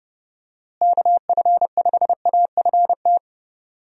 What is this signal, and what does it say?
Morse code, heard as a keyed tone, about 30 wpm, 700 Hz. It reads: KF5AFT
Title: USA Call Signs - 6 Characters - Repeated 3x - 30wpm